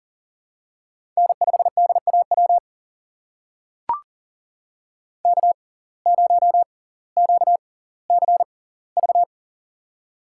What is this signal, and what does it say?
Morse code, heard as a keyed tone, about 38 wpm, spaced as for 20 wpm, 700 Hz. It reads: N5DAW K0QCV
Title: USA Call Signs - 5 Characters - 6x Character Spacing - 40wpm